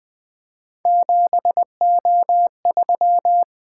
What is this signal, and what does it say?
Morse code, heard as a keyed tone, about 20 wpm, 700 Hz. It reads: 7O3
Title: Three Letter-Number Combinations - 20wpm